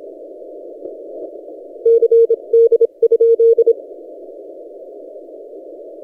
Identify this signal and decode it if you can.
Morse code, heard as a keyed tone, about 27 wpm, 460 Hz. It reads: CD?
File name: posts / t32rc-30cw